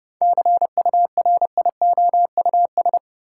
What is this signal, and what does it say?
Morse code, heard as a keyed tone, about 30 wpm, 700 Hz. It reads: CURIOUS
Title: Seven Letter Words - 30wpm